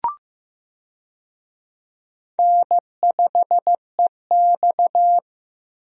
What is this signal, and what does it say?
Morse code, heard as a keyed tone, about 15 wpm, 700 Hz. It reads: N5EX
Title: USA Call Signs - 4 Characters - Repeated 2x - 15wpm